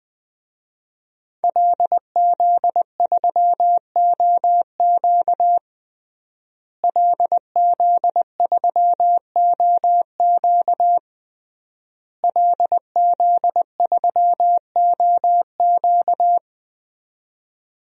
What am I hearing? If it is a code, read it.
Morse code, heard as a keyed tone, about 21 wpm, 700 Hz. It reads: LZ3OQ LZ3OQ LZ3OQ
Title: DXCC-100 Call Signs - 5 Characters - Repeated 3x - 20wpm